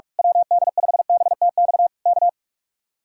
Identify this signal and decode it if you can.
Morse code, heard as a keyed tone, about 46 wpm, 700 Hz. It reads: WD5BTX K
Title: Speed-Racing Common QSO Phrases - 45wpm